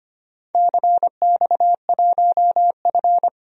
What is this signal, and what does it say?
Morse code, heard as a keyed tone, about 25 wpm, 700 Hz. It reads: CX1F
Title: DXCC-100 Call Signs - 4 Characters - 25wpm